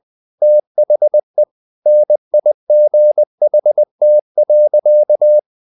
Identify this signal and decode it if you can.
Morse code, heard as a keyed tone, about 20 wpm, 600 Hz. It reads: THE NIGHT.
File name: the old man and the sea / the old man and the sea part 1 20wpm